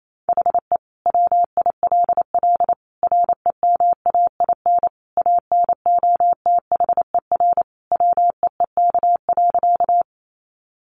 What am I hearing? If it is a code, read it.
Morse code, heard as a keyed tone, about 28 wpm, 700 Hz. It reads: HE WILL REMAIN ANOTHER WEEK.
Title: Sentences from Top 1200 Words 28wpm